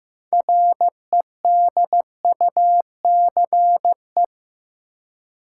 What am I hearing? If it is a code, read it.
Morse code, heard as a keyed tone, about 16 wpm, 700 Hz. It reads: REDUCE
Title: Six Letter Words - 15wpm